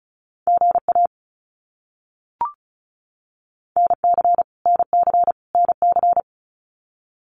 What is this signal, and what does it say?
Morse code, heard as a keyed tone, about 34 wpm, 700 Hz. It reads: GA NC NC NC
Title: US States Repeated 3x 35wpm